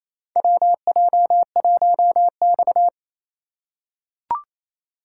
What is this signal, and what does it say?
Morse code, heard as a keyed tone, about 28 wpm, 700 Hz. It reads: WJ1X
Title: USA Call Signs - 4 Characters - 4x Character Spacing - 28wpm